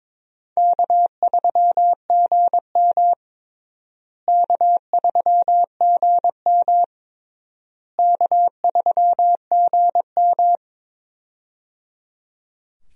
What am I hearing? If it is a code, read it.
Morse code, heard as a keyed tone, about 22 wpm, 700 Hz. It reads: K3GM K3GM K3GM
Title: USA Call Signs - 4 Characters - Repeated 3x - 22wpm